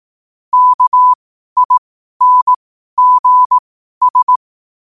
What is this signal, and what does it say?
Morse code, heard as a keyed tone, about 18 wpm, spaced as for 12 wpm, 1000 Hz. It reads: KINGS
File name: qotdfiles / 12wpm